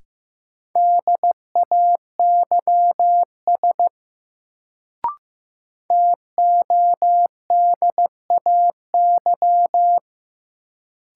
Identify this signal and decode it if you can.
Morse code, heard as a keyed tone, about 15 wpm, 700 Hz. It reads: DAYS TODAY
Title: Sets of 5 Words Encore - New Word Drill - 15wpm